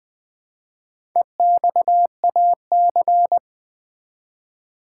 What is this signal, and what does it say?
Morse code, heard as a keyed tone, about 20 wpm, 700 Hz. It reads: EXAC
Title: Build A Word - 1 Letter At A Time - 5-Letter Words - 20wpm